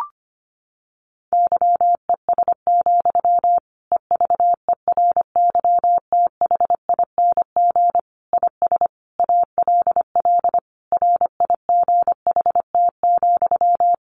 Morse code, heard as a keyed tone, about 25 wpm, 700 Hz. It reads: YES, EVERYTHING IS ALL RIGHT,
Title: Sentences from Top 1100 Words 25wpm